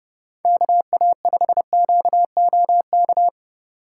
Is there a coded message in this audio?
Yes: Morse code from 0.5 to 3.3 s, heard as a keyed tone, about 30 wpm, 700 Hz. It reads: KA5QOK